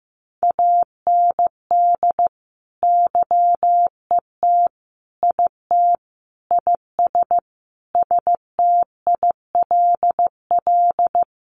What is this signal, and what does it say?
Morse code, heard as a keyed tone, about 15 wpm, 700 Hz. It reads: AND YET IT IS STILL
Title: Sentences from Top 1500 Words 15wpm